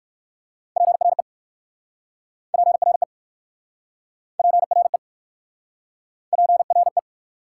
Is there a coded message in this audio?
Yes: Morse code from 0.8 to 7.0 s, heard as a keyed tone, about 62 wpm, 690 Hz. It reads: PRE PRE PRE PRE